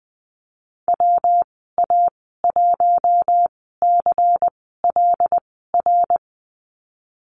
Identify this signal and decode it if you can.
Morse code, heard as a keyed tone, about 20 wpm, spaced as for 14 wpm, 700 Hz. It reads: WA1CLR
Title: 154 Lesson 25g - Letter p - Call Signs 20wpm